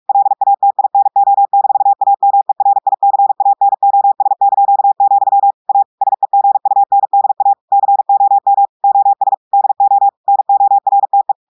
Morse code, heard as a keyed tone, about 46 wpm, 800 Hz. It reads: LATINO-AMERICANOS), A SEGUNDA COM OS DO NORTE
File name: portugais texte en clair 1 vitesse 45 mots minutes